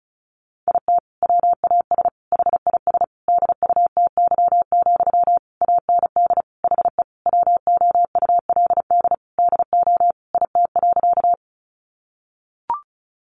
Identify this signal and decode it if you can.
Morse code, heard as a keyed tone, about 34 wpm, 700 Hz. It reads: IT WAS HIS DUTY, AND HE WOULD DO IT.
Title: Sentences from Top 1500 Words 35wpm